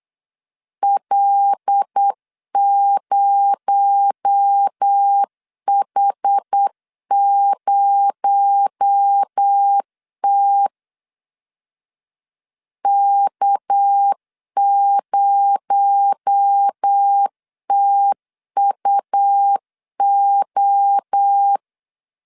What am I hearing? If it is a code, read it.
Morse code, heard as a keyed tone, about 8 wpm, 800 Hz. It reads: L0H0T K0TUO
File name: LEZIONI / 6 PKX0 / 6 PKX0 8wpm 3341